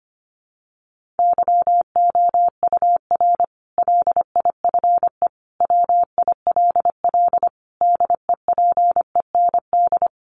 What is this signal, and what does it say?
Morse code, heard as a keyed tone, about 25 wpm, 700 Hz. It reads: YOUR LIFE WILL DEPEND
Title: Sentences from Top 1300 Words 25wpm